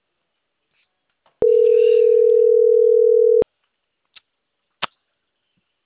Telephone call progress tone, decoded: ringback tone

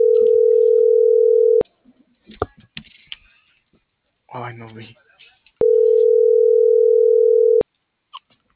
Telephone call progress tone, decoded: ringback tone

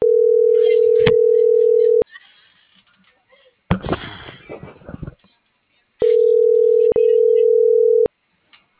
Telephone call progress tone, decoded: ringback tone

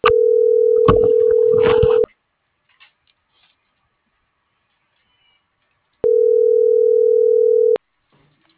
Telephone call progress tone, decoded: ringback tone

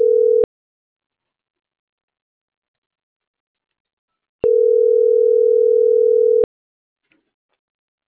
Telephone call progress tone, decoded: ringback tone

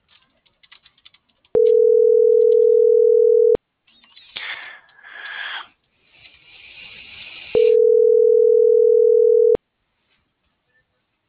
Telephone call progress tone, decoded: ringback tone

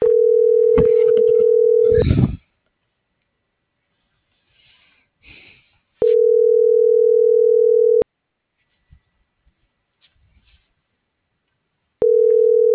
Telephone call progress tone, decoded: ringback tone

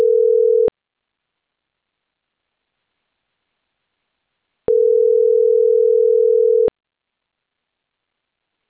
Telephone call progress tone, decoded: ringback tone

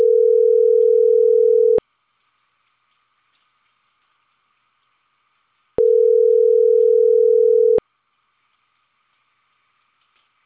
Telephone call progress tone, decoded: ringback tone

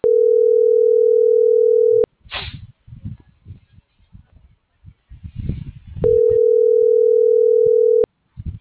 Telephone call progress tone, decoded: ringback tone

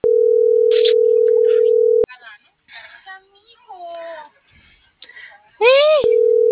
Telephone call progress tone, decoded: ringback tone

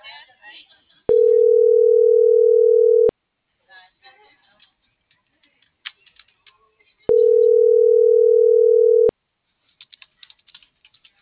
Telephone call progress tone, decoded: ringback tone

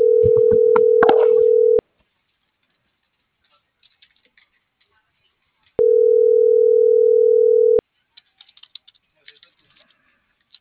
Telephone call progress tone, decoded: ringback tone

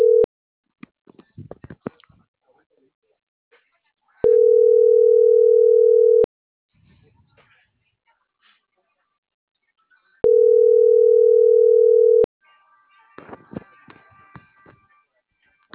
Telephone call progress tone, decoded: ringback tone

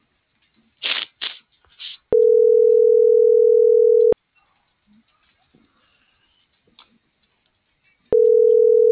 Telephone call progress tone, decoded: ringback tone

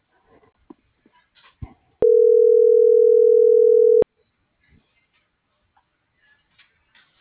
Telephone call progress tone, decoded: ringback tone

